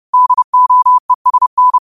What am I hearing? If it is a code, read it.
Morse code, heard as a keyed tone, about 30 wpm, 1000 Hz. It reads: DOESN
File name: qotdfiles / 30wpm